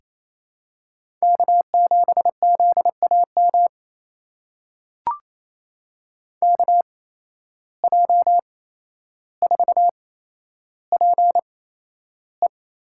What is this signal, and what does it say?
Morse code, heard as a keyed tone, about 28 wpm, spaced as for 12 wpm, 700 Hz. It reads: K7ZAM KJ4PE
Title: USA Call Signs - 5 Characters - 8x Character Spacing - 28wpm